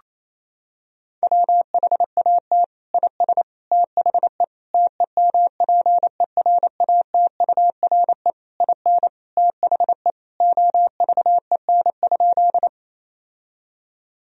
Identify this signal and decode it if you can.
Morse code, heard as a keyed tone, about 28 wpm, 700 Hz. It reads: WHAT IS THE TEMPERATURE IN THE OVEN?
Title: Sentences from Top 1000 Words - Review Part 4 - 28wpm